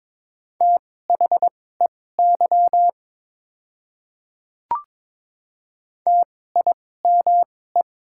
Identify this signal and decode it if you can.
Morse code, heard as a keyed tone, about 22 wpm, spaced as for 16 wpm, 700 Hz. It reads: THEY TIME
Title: 2x Character Spacing - Top 200 Words - 22wpm